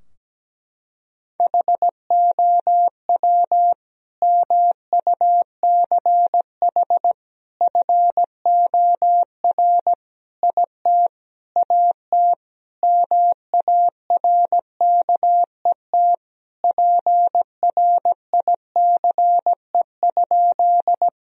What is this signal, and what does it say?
Morse code, heard as a keyed tone, about 17 wpm, 700 Hz. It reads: HOW MUCH FOR IT AT MARKET PRICE?
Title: Sentences from Top 1000 Words - Review Part 3 - 17wpm